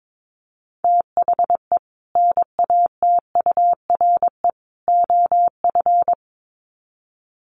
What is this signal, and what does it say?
Morse code, heard as a keyed tone, about 22 wpm, 700 Hz. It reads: THE NATURE OF
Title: Sets of 3 Words 22wpm